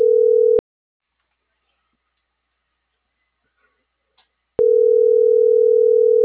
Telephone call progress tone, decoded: ringback tone

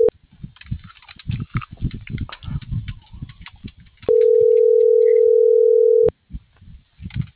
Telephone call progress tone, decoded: ringback tone

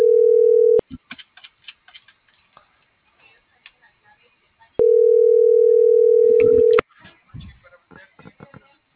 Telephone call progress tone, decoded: ringback tone

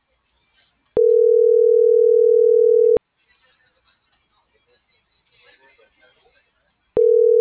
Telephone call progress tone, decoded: ringback tone

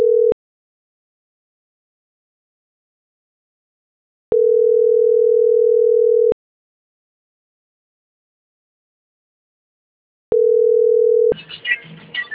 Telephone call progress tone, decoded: ringback tone

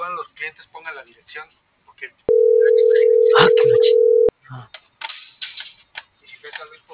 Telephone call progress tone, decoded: ringback tone

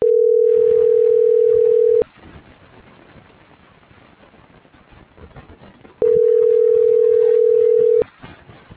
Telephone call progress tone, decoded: ringback tone

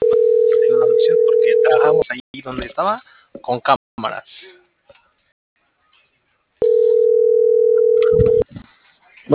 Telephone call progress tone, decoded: ringback tone

